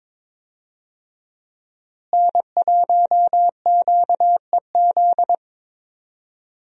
Morse code, heard as a keyed tone, about 22 wpm, 700 Hz. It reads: N1QEZ
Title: USA Call Signs - 5 Characters - Repeated 3x - 22wpm